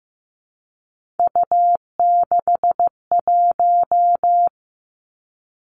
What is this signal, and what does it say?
Morse code, heard as a keyed tone, about 15 wpm, 700 Hz. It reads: U61
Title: Three Challenging Letter-Number Combinations 15wpm